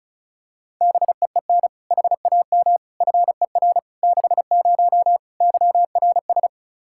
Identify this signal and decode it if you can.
Morse code, heard as a keyed tone, about 36 wpm, 700 Hz. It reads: BEEN HAM FER 60 YRS
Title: K1USN SST QSO Practice 35wpm